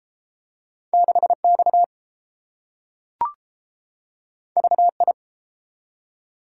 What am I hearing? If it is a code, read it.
Morse code, heard as a keyed tone, about 30 wpm, 700 Hz. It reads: 6 X V I